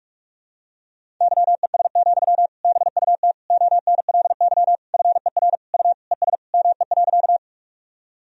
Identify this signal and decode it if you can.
Morse code, heard as a keyed tone, about 46 wpm, 690 Hz. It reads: YES, BUT ONLY FER U ES ME.
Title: Sentences from Top 500 Words and Abbreviations 45wpm